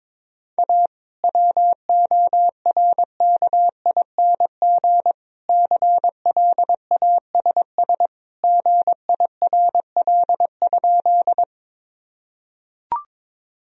Morse code, heard as a keyed tone, about 21 wpm, 700 Hz. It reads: A WORKING CLASS GIRL?
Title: Sentences from Top 1000 Words - Review Part 4 - 22wpm